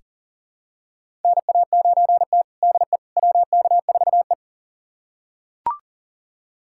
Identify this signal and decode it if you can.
Morse code, heard as a keyed tone, about 40 wpm, 700 Hz. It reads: NA9T DE WK4E
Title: USA Callsign DE Callsign - 4 Characters - 40wpm